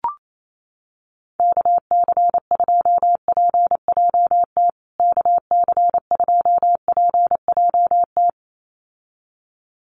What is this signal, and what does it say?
Morse code, heard as a keyed tone, about 28 wpm, 700 Hz. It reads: KC2PJT KC2PJT
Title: USA Call Signs Repeated 2x 28wpm